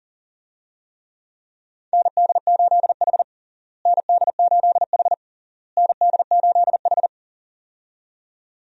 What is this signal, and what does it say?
Morse code, heard as a keyed tone, about 40 wpm, 700 Hz. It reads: ND8H ND8H ND8H
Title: USA Call Signs - 4 Characters - Repeated 3x - 40wpm